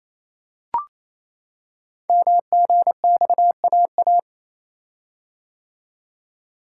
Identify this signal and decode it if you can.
Morse code, heard as a keyed tone, about 28 wpm, 700 Hz. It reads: MGXAA
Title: Code Groups - 5 Letters - 28wpm